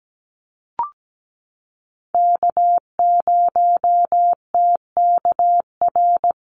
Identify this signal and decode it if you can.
Morse code, heard as a keyed tone, about 17 wpm, 700 Hz. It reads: K0TKR
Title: USA Call Signs Repeated 2x 17wpm